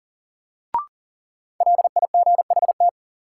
Morse code, heard as a keyed tone, about 40 wpm, 700 Hz. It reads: LIGHT
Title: Sentences from Top 200 Words - New Word Drill - 40wpm